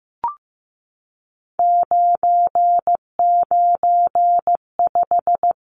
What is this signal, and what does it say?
Morse code, heard as a keyed tone, about 15 wpm, 700 Hz. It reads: 995
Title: Numbers - 3-Digit 15wpm